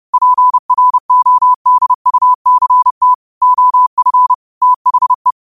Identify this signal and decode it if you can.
Morse code, heard as a keyed tone, about 30 wpm, 1000 Hz. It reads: PRODUCT OF THE
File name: qotdfiles / 30wpm